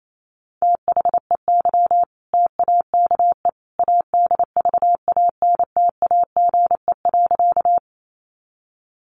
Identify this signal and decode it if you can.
Morse code, heard as a keyed tone, about 28 wpm, 700 Hz. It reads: THEY TAKE ADVANTAGE.
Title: Sentences from Top 900 Words 28wpm